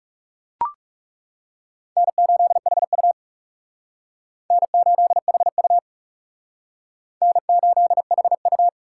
Morse code, heard as a keyed tone, about 39 wpm, 700 Hz. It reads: N8HU N8HU N8HU